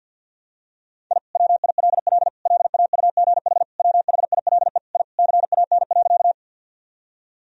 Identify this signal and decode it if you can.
Morse code, heard as a keyed tone, about 50 wpm, 700 Hz. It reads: I WILL LAUGH WHILE I CAN.
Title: Sentences from Top 1000 Words - Review Part 1 - 50wpm